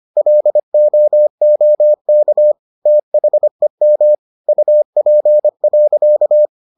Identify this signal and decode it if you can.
Morse code, heard as a keyed tone, about 25 wpm, 600 Hz. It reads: LOOK THEM UP.